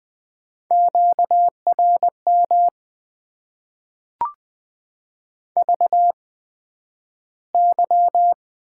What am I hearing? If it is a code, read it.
Morse code, heard as a keyed tone, about 20 wpm, spaced as for 11 wpm, 700 Hz. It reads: QRM VY
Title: Top 90 QSO Elements - 8x Character Spacing - 20wpm